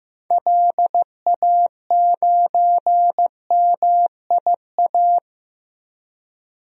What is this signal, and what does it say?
Morse code, heard as a keyed tone, about 15 wpm, 700 Hz. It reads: LA9MIA
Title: DXCC-100 Call Signs - 6 Characters - 15wpm